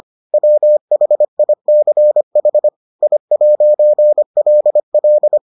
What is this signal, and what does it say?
Morse code, heard as a keyed tone, about 25 wpm, 600 Hz. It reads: WHICH I'LL